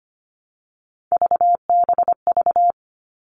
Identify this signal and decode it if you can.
Morse code, heard as a keyed tone, about 25 wpm, 700 Hz. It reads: VBV
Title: All Cut Numbers - 3-Digit 25wpm